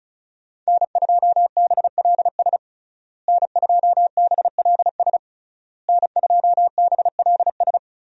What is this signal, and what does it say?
Morse code, heard as a keyed tone, about 34 wpm, 700 Hz. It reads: N2BLS N2BLS N2BLS
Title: USA Call Signs - 5 Characters - Repeated 3x - 35wpm